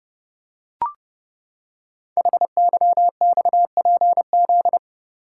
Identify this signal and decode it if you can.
Morse code, heard as a keyed tone, about 30 wpm, 700 Hz. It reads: HYXPZ
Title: Code Groups - 5 Letters - 30wpm